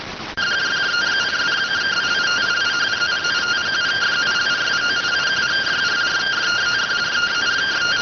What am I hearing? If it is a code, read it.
Two-tone teleprinter signal, about 170 Hz shift.